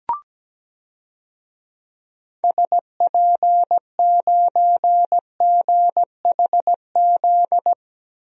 Morse code, heard as a keyed tone, about 17 wpm, 700 Hz. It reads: SP9GHZ
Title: DXCC-100 Call Signs - 6 Characters - Repeated 2x - 17wpm